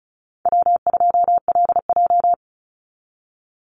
Morse code, heard as a keyed tone, about 34 wpm, 700 Hz. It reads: W2LJ
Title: Call Signs - 35wpm